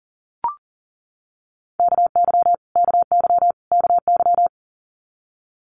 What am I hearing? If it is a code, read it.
Morse code, heard as a keyed tone, about 40 wpm, 700 Hz. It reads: KY KY KY
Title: Multi-letter Phonemes Repeated 3x 40wpm